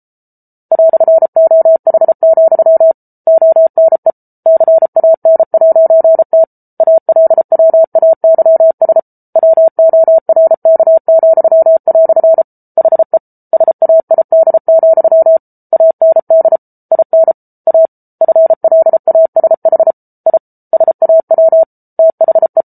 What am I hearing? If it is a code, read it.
Morse code, heard as a keyed tone, about 34 wpm, 650 Hz. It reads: "OH, ONE CAN'T ALWAYS WORK," HE SAID, AND IN A FLASH I SAW THE